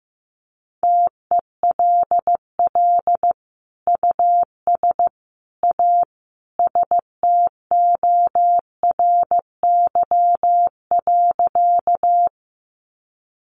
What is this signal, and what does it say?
Morse code, heard as a keyed tone, about 15 wpm, 700 Hz. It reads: TELL US A STORY.
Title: Sentences from Top 300 Words 15wpm